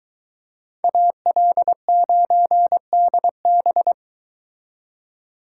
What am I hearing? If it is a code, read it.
Morse code, heard as a keyed tone, about 22 wpm, 700 Hz. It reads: AL9DB